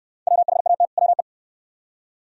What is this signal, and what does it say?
Morse code, heard as a keyed tone, about 67 wpm, 690 Hz. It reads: WHAT WE